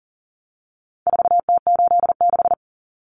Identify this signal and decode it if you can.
Morse code, heard as a keyed tone, about 40 wpm, 700 Hz. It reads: 4T86
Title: Common Cut Numbers - 4-Digit 40wpm